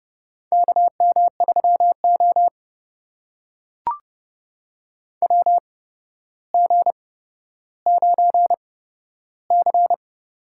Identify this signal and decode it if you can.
Morse code, heard as a keyed tone, about 30 wpm, spaced as for 13 wpm, 700 Hz. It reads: KM3O WG9C
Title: USA Call Signs - 4 Characters - 8x Character Spacing - 30wpm